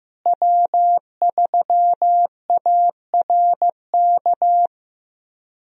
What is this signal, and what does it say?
Morse code, heard as a keyed tone, about 15 wpm, 700 Hz. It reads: W3ARK
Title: USA Call Signs - 5 Characters - Repeated 2x - 15wpm